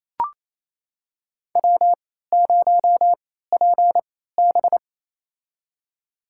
Morse code, heard as a keyed tone, about 28 wpm, spaced as for 16 wpm, 700 Hz. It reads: W0PB
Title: USA Call Signs - 4 Characters - 3x Character Spacing - 28wpm